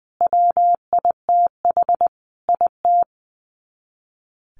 Morse code, heard as a keyed tone, about 20 wpm, 700 Hz. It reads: WITH IT
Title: Sets of 5 Words Encore 20wpm